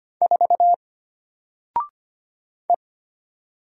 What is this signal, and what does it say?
Morse code, heard as a keyed tone, about 25 wpm, 700 Hz. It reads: 4 E
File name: Single-Letter-Number-25wpm